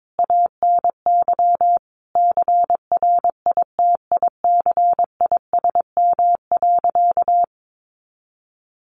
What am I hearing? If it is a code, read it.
Morse code, heard as a keyed tone, about 22 wpm, 700 Hz. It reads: ANY CRITICISM.
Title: Sentences from Top 1300 Words 22wpm